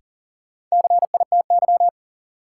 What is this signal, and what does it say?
Morse code, heard as a keyed tone, about 40 wpm, 700 Hz. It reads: CITY
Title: Sets of 2 Words - New Word Drill - 40wpm